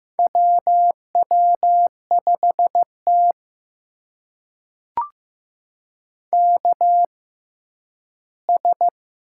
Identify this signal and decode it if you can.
Morse code, heard as a keyed tone, about 15 wpm, spaced as for 10 wpm, 700 Hz. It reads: WW5T KS
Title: USA Call Signs - 6x Character Spacing - 15wpm